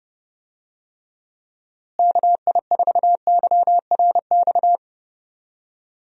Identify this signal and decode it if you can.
Morse code, heard as a keyed tone, about 30 wpm, 700 Hz. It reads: KI4YRX
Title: USA Call Signs - 6 Characters - Repeated 2x - 30wpm